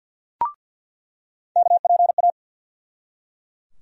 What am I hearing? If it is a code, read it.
Morse code, heard as a keyed tone, about 46 wpm, 700 Hz. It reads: KPA
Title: USA Call Sign Postfixes - 3 Letters 50wpm